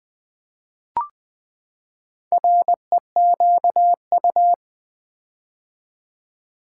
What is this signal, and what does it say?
Morse code, heard as a keyed tone, about 20 wpm, 700 Hz. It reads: REQU
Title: Build A Word - 4 Letters At A Time - 11-Letter Words - 20wpm